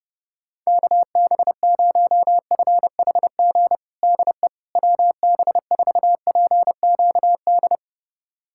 Morse code, heard as a keyed tone, about 30 wpm, 700 Hz. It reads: KB0FHG DE WB4PQD
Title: USA Callsign DE Callsign - 6 Characters - 30wpm